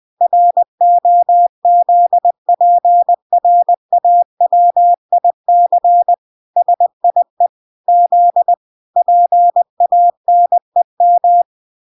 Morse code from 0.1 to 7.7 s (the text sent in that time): ROZPRAWIC SIE